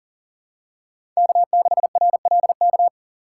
Morse code, heard as a keyed tone, about 40 wpm, 700 Hz. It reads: K6RLK
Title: USA Call Signs - 5 Characters - 2x Character Spacing - 40wpm